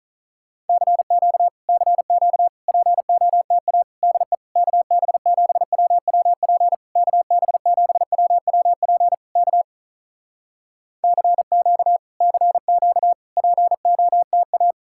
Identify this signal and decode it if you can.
Morse code, heard as a keyed tone, about 44 wpm, 700 Hz. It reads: CQ CQ POTA DE KB7WWP KB7WWP K CQ CQ POTA
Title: Speed-Racing Common QSO Phrases Encore - 35wpm